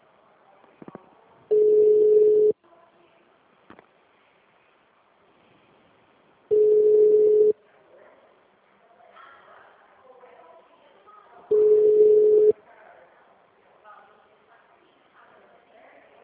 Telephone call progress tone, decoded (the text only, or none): ringback tone